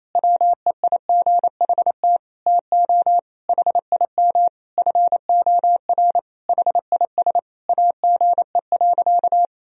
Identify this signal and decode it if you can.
Morse code, heard as a keyed tone, about 28 wpm, 700 Hz. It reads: WEIGHT TO HIM FOR HIS AGE.
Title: Sentences from Top 1000 Words - Review Part 3 - 28wpm